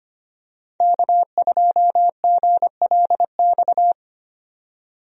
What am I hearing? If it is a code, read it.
Morse code, heard as a keyed tone, about 25 wpm, 700 Hz. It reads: K2GLX
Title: USA Call Signs - 5 Characters - 25wpm